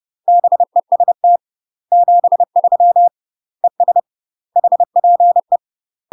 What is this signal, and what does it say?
Morse code, heard as a keyed tone, about 30 wpm, 700 Hz. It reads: BEST 73 ES HPE
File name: UK MASTER QSOs TRAINING / 30WPM / UK MASTER QSOs 0028